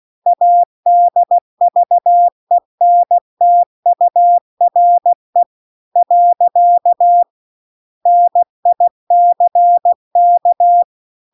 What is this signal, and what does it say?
Morse code, heard as a keyed tone, about 16 wpm, 700 Hz. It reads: ADVENTURE . NICK
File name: CW-The-Big-Twoheaded-River-2-16WPM